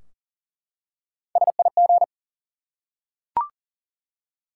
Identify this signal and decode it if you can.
Morse code, heard as a keyed tone, about 40 wpm, 700 Hz. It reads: SIG
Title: Top 90 QSO Elements 40wpm